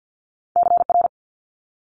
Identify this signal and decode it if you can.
Morse code, heard as a keyed tone, about 50 wpm, 700 Hz. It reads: CR
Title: Multi-letter Phonemes Quick Fire 50wpm